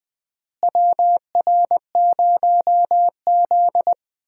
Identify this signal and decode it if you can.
Morse code, heard as a keyed tone, about 20 wpm, 700 Hz. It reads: WR0Z
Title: USA Call Signs - 4 Characters - Repeated 2x - 20wpm